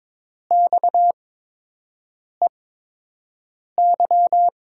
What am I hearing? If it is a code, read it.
Morse code, heard as a keyed tone, about 22 wpm, spaced as for 6 wpm, 700 Hz. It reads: XEY
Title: USA Call Signs - 5 Characters - 8x Character Spacing - 22wpm